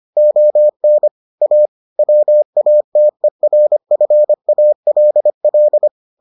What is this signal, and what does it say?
Morse code, heard as a keyed tone, about 25 wpm, 600 Hz. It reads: ON A WATERFALL